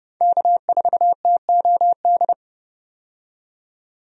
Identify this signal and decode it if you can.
Morse code, heard as a keyed tone, about 30 wpm, 700 Hz. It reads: K4TOD
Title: USA Call Signs - 5 Characters - 30wpm